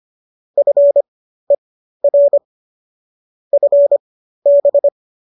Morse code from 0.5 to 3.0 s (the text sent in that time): FER